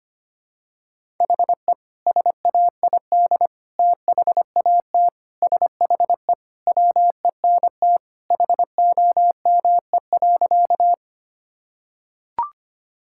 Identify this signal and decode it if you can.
Morse code, heard as a keyed tone, about 25 wpm, 690 Hz. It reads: HE SAID THAT SHE WENT HOME.
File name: practice / 25wpm / Sentences with 2-Word Phrases - 25wpm